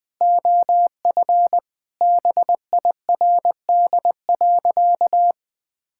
Morse code, heard as a keyed tone, about 20 wpm, 700 Hz. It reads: OF BIRD.
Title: Sentences with 3-Word Phrases - 20wpm